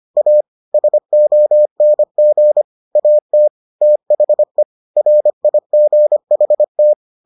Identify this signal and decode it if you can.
Morse code, heard as a keyed tone, about 25 wpm, 600 Hz. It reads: A SONG AT THE RIGHT